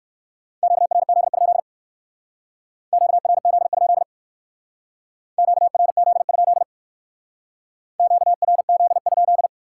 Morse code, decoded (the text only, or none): QRZ? QRZ? QRZ? QRZ?